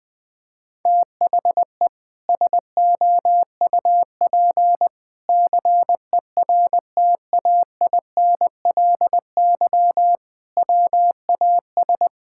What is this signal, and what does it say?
Morse code, heard as a keyed tone, about 21 wpm, 700 Hz. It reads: THE SOUP CERTAINLY WAS
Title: Sentences from Top 1000 Words - Review Part 6 - 20wpm